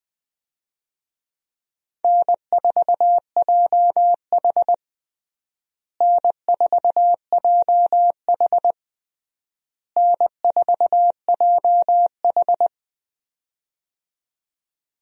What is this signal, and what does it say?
Morse code, heard as a keyed tone, about 20 wpm, 700 Hz. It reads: N4JH N4JH N4JH
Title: USA Call Signs - 4 Characters - Repeated 3x - 20wpm